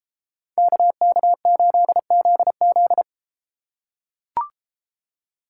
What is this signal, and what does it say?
Morse code, heard as a keyed tone, about 31 wpm, 700 Hz. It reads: KK8ZZ